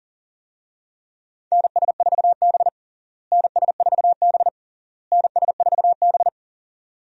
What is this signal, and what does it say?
Morse code, heard as a keyed tone, about 40 wpm, 700 Hz. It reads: NS4B NS4B NS4B
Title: USA Call Signs - 4 Characters - Repeated 3x - 40wpm